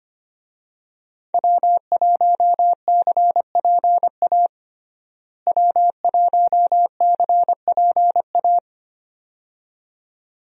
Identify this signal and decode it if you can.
Morse code, heard as a keyed tone, about 25 wpm, 700 Hz. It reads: W1CPA W1CPA
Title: USA Call Signs - 5 Characters - Repeated 2x - 25wpm